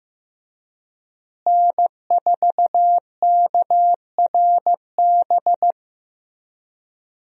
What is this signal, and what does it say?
Morse code, heard as a keyed tone, about 15 wpm, 700 Hz. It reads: N4KRB
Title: USA Call Signs - 5 Characters - Repeated 2x - 15wpm